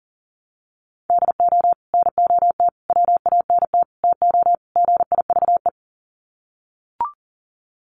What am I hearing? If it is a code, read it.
Morse code, heard as a keyed tone, about 40 wpm, 700 Hz. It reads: DO NOT WANT TO GIVE
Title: Sets of 5 Words Encore 40wpm